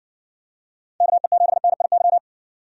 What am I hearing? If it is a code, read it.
Morse code, heard as a keyed tone, about 55 wpm, 700 Hz. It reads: KE7NIY